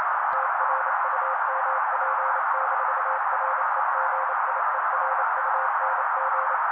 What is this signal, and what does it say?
Morse code, heard as a keyed tone, about 28 wpm, 530 Hz. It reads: TRUMP-REGIERUNG
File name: daily / 27wpm noise